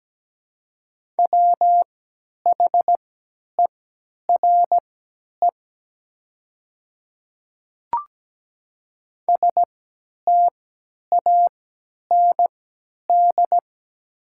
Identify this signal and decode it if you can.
Morse code, heard as a keyed tone, about 17 wpm, spaced as for 10 wpm, 690 Hz. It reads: WHERE STAND
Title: 3x Character Spacing - Top 300 Words - 17wpm